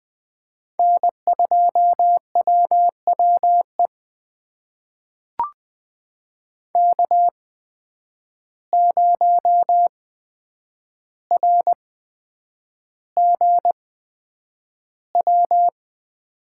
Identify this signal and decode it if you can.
Morse code, heard as a keyed tone, about 20 wpm, spaced as for 9 wpm, 700 Hz. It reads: N2WWE K0RGW
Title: USA Call Signs - 5 Characters - 8x Character Spacing - 20wpm